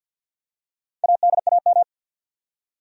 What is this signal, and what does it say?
Morse code, heard as a keyed tone, about 50 wpm, 700 Hz. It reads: ADAM
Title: USA Common Names - Max 4 Letters - 50wpm